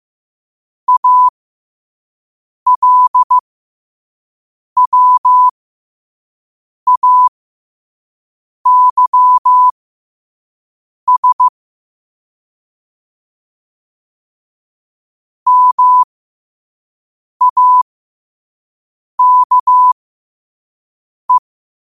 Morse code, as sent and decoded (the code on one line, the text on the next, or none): .- .-.. .-- .- -.-- ... / -- .- -.- .
ALWAYS MAKE